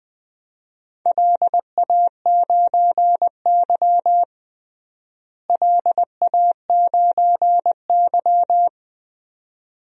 Morse code, heard as a keyed tone, about 20 wpm, 700 Hz. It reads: LA9Y LA9Y
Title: DXCC-100 Call Signs - 4 Characters - Repeated 2x - 20wpm